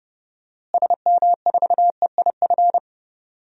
Morse code, heard as a keyed tone, about 31 wpm, 700 Hz. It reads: SM4EIF
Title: DXCC-100 Call Signs - 6 Characters - Repeated 3x - 30wpm